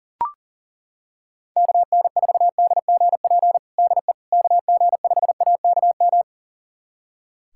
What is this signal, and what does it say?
Morse code, heard as a keyed tone, about 37 wpm, 700 Hz. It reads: KN4DGP DE KG5AKM
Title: USA Callsign DE Callsign - 6 Characters - 40wpm